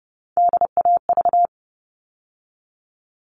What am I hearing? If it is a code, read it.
Morse code, heard as a keyed tone, about 30 wpm, 700 Hz. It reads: DAV
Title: All Cut Numbers - 3-Digit 30wpm